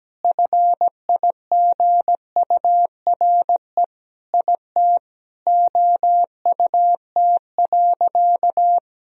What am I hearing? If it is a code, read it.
Morse code, heard as a keyed tone, about 17 wpm, 700 Hz. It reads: FIGURE IT OUT.
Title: Sentences from Top 1000 Words - Review Part 2 - 17wpm